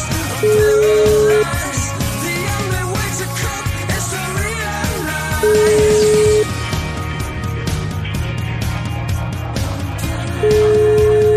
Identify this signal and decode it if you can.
Ringback tone (Europe, 425 Hz), with noise.